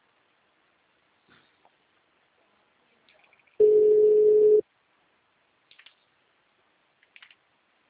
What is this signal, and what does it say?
Ringback tone (Europe, 425 Hz).
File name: out- -2033- - - .0